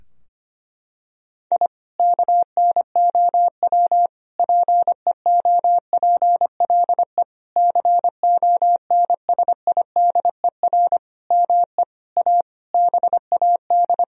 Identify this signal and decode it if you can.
Morse code, heard as a keyed tone, about 25 wpm, 700 Hz. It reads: I KNOW PEOPLE CONSIDER ME A BAD
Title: Sentences from Top 1000 Words - Review Part 4 - 25wpm